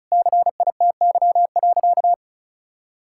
Morse code, heard as a keyed tone, about 34 wpm, 700 Hz. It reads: CITY.
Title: Sentences from Top 1600 Words 35wpm